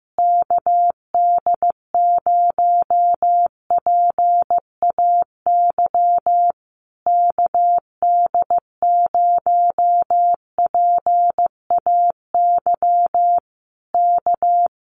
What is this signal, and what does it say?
Morse code, heard as a keyed tone, about 15 wpm, 700 Hz. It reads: KD0PAY KD0PAY K
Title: Common QSO Phrases Encore 15wpm